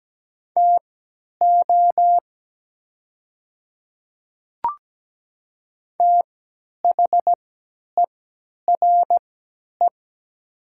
Morse code, heard as a keyed tone, about 17 wpm, spaced as for 10 wpm, 700 Hz. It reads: TO THERE
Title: 3x Character Spacing - Top 100 Words - 17wpm